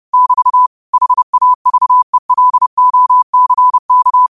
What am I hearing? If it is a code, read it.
Morse code, heard as a keyed tone, about 30 wpm, 1000 Hz. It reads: <BT> HAVELOCK
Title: 30 WPM morse code quotes for Fri, 01 Mar 2024